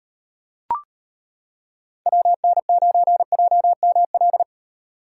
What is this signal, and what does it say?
Morse code, heard as a keyed tone, about 32 wpm, 700 Hz. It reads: WN9JML